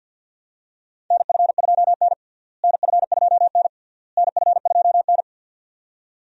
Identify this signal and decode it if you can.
Morse code, heard as a keyed tone, about 51 wpm, 700 Hz. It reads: NF2N NF2N NF2N
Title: USA Call Signs - 4 Characters - Repeated 3x - 50wpm